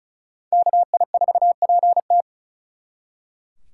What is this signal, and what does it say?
Morse code, heard as a keyed tone, about 45 wpm, 700 Hz. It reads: KI4PT